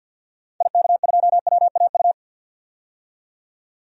Morse code, heard as a keyed tone, about 48 wpm, 700 Hz. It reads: IK2WAU
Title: DXCC-100 Call Signs - 6 Characters - 50wpm